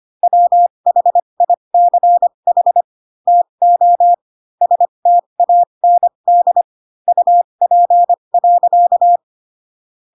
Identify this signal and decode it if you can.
Morse code, heard as a keyed tone, about 25 wpm, spaced as for 22 wpm, 700 Hz. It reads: WHICH TO STAND UP.
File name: Pinocchio / Pinocchio ch2 18 25 ik1hge